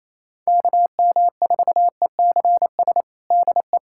Morse code, heard as a keyed tone, about 27 wpm, 700 Hz. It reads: KM4ECS DE
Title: USA Callsign DE Callsign - 6 Characters - 28wpm